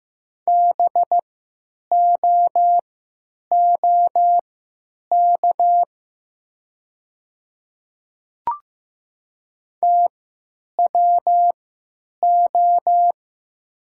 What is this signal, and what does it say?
Morse code, heard as a keyed tone, about 15 wpm, spaced as for 9 wpm, 700 Hz. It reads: BOOK TWO
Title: 3x Character Spacing - Top 300 Words - 15wpm